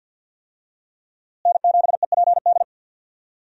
Morse code, heard as a keyed tone, about 47 wpm, 700 Hz. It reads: N7EPD
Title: USA Call Signs - 5 Characters - Repeated 2x - 50wpm